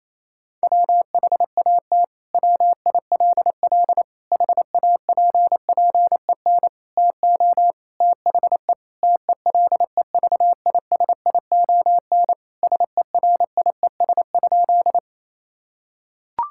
Morse code, heard as a keyed tone, about 28 wpm, 700 Hz. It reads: WHAT WILL HAPPEN TO THE TELEVISION SERIES?
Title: Sentences from Top 1000 Words - Review Part 5 - 28wpm